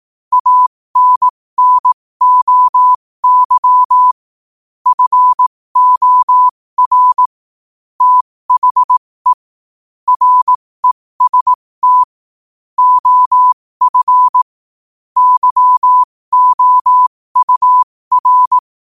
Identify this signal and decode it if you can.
Morse code, heard as a keyed tone, about 18 wpm, spaced as for 15 wpm, 1000 Hz. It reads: ANNOY FOR THE REST OF YOUR
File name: 15wpm